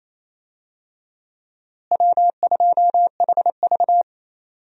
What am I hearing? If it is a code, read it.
Morse code, heard as a keyed tone, about 28 wpm, 700 Hz. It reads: W2HV
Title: USA Call Signs - 4 Characters - Repeated 3x - 28wpm